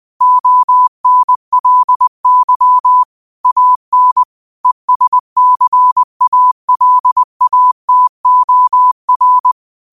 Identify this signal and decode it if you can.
Morse code, heard as a keyed tone, about 20 wpm, 1000 Hz. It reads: ONLY AN ESCALATOR
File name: qotdfiles / 20wpm